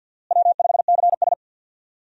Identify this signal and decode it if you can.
Morse code, heard as a keyed tone, about 52 wpm, 700 Hz. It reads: W5CS
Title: USA Call Signs - 4 Characters - Repeated 2x - 50wpm